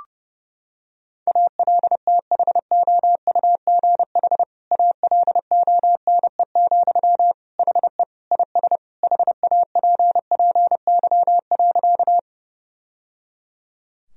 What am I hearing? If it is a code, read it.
Morse code, heard as a keyed tone, about 30 wpm, 700 Hz. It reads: ALTHOUGH ALONE, HE IS HAPPY.
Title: Sentences from Top 1000 Words - Review Part 3 - 30wpm